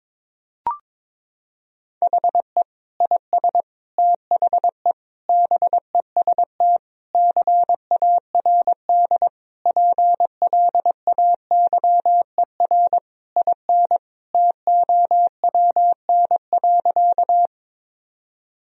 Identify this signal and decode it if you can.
Morse code, heard as a keyed tone, about 22 wpm, 700 Hz. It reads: HE IS THE BEST CARD PLAYER IN TOWN.
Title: Sentences from Top 1000 Words - Review Part 4 - 22wpm